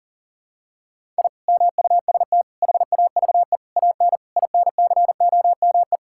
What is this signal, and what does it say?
Morse code, heard as a keyed tone, about 40 wpm, 700 Hz. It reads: I MUST HAVE AN INCOME
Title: Sentences from Top 1000 Words - Review Part 5 - 40wpm